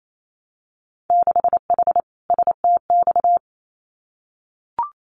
Morse code, heard as a keyed tone, about 28 wpm, 700 Hz. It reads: 6H STX
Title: Winter Field Day Exchanges Encore 28wpm